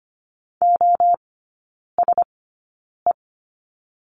Morse code, heard as a keyed tone, about 25 wpm, spaced as for 9 wpm, 690 Hz. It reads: OSE
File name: Lesson 14e - Number 2 - Three Characters 25wpm